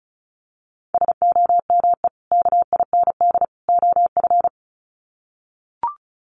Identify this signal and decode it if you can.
Morse code, heard as a keyed tone, about 34 wpm, 700 Hz. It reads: SOME KIND OF
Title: Sets of 3 Words 35wpm